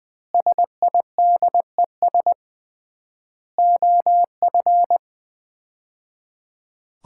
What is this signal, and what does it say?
Morse code, heard as a keyed tone, about 21 wpm, 700 Hz. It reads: SIDES OF
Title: 3x Word-Spacing - Sets of 4 Words - 20wpm